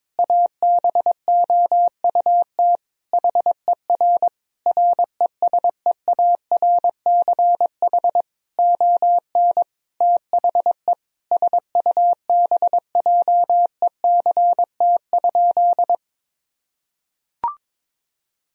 Morse code, heard as a keyed tone, about 22 wpm, 700 Hz. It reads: ABOUT HER RESEARCH ON THE SUBJECT?
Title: Sentences from Top 1000 Words - Review Part 4 - 22wpm